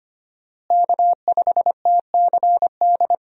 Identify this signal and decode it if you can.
Morse code, heard as a keyed tone, about 25 wpm, 700 Hz. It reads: K5TCD
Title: USA Call Signs - 5 Characters - Repeated 2x - 25wpm